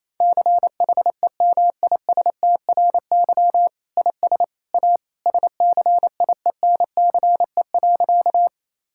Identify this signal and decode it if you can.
Morse code, heard as a keyed tone, about 27 wpm, 700 Hz. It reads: CHEMISTRY IS A SCIENCE.
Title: Sentences from Top 1000 Words - Review Part 6 - 28wpm